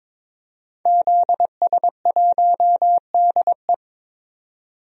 Morse code, heard as a keyed tone, about 23 wpm, 700 Hz. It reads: ZS1DE
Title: DXCC-100 Call Signs - 5 Characters - 22wpm